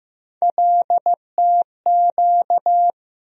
Morse code, heard as a keyed tone, about 15 wpm, 700 Hz. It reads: LTQ
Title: USA Call Sign Postfixes - 3 Letters 15wpm